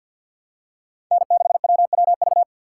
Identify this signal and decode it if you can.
Morse code, heard as a keyed tone, about 48 wpm, 700 Hz. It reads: N6WWV
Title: USA Call Signs - 5 Characters - 3x Character Spacing - 50wpm